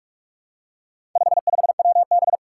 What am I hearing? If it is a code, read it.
Morse code, heard as a keyed tone, about 46 wpm, 700 Hz. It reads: 55WB
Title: Four Challenging Letter-Number Combinations 45wpm